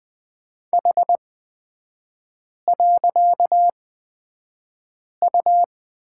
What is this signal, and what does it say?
Morse code, heard as a keyed tone, about 20 wpm, spaced as for 5 wpm, 690 Hz. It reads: H.U
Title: 255 Lesson 41e - Period - Three Characters 20wpm